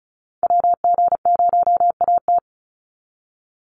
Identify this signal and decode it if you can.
Morse code, heard as a keyed tone, about 34 wpm, 700 Hz. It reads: WG0AT
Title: NA SOTA Activator Callsigns 35wpm